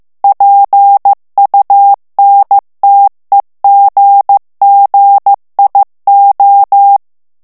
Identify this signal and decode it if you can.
Morse code, heard as a keyed tone, about 15 wpm, 800 Hz. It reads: PUNTEGGIO
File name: morse mp3 / ITALIEN texte clair 2 vitesse 14 mots minute